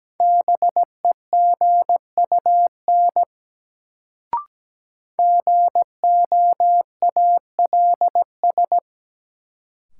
Morse code, heard as a keyed tone, about 17 wpm, 700 Hz. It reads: BEGUN GOALS
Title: Sentences from Top 1000 Words - Review Part 6 - New Word Drill - 17wpm